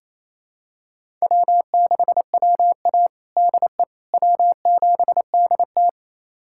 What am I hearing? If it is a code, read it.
Morse code, heard as a keyed tone, about 28 wpm, 700 Hz. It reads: W6WA DE W7DT
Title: USA Callsign DE Callsign - 4 Characters - 28wpm